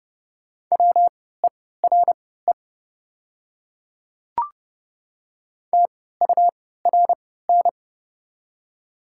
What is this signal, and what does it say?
Morse code, heard as a keyed tone, about 31 wpm, spaced as for 17 wpm, 700 Hz. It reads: WERE TURN
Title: 3x Character Spacing - Top 300 Words - 30wpm